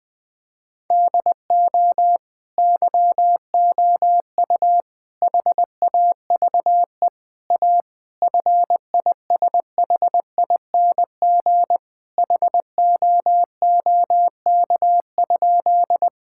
Morse code, heard as a keyed tone, about 20 wpm, 700 Hz. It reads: DO YOU HAVE A FISHING HOOK?
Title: Sentences from Top 1600 Words 20wpm